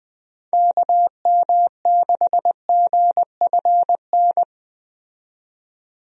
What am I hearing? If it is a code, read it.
Morse code, heard as a keyed tone, about 20 wpm, 700 Hz. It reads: KM6GFN
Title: USA Call Signs - 6 Characters - 20wpm